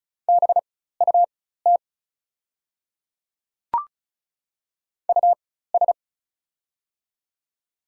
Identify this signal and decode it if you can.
Morse code, heard as a keyed tone, about 36 wpm, spaced as for 16 wpm, 700 Hz. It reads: BUT US
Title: 4x Character Spacing - Top 200 Words - 35wpm